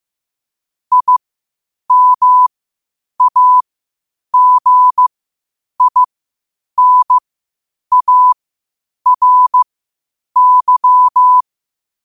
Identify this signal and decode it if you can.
Morse code, heard as a keyed tone, about 15 wpm, spaced as for 8 wpm, 1000 Hz. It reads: IMAGINARY
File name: qotdfiles / 8wpm